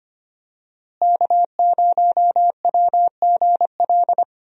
Morse code, heard as a keyed tone, about 25 wpm, 700 Hz. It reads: K0WGL
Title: USA Call Signs - 5 Characters - Repeated 3x - 25wpm